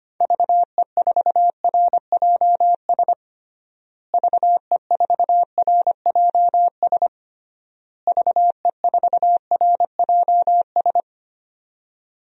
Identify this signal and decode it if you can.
Morse code, heard as a keyed tone, about 25 wpm, 700 Hz. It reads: VE4RJS VE4RJS VE4RJS
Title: DXCC-100 Call Signs - 6 Characters - Repeated 3x - 25wpm